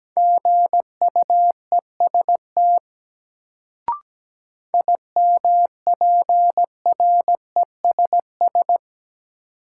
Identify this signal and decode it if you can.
Morse code, heard as a keyed tone, about 17 wpm, 700 Hz. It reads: GUEST IMPRESS
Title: Sentences from Top 1600 Words - New Word Drill Alt - 17wpm